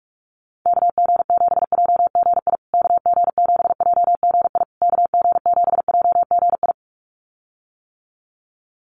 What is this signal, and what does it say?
Morse code, heard as a keyed tone, about 44 wpm, 700 Hz. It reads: KG7JGI KG7JGI KG7JGI
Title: USA Call Signs Repeated 3x 45wpm